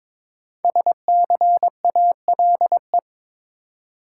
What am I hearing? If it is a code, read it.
Morse code, heard as a keyed tone, about 22 wpm, 700 Hz. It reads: SCALE
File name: 5 Letter Words - First 3 Letters Given - 22wpm